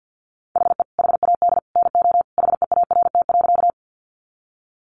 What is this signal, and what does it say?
Morse code, heard as a keyed tone, about 50 wpm, 700 Hz. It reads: HE HAD NO HEART.
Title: Sentences from Top 1200 Words 50wpm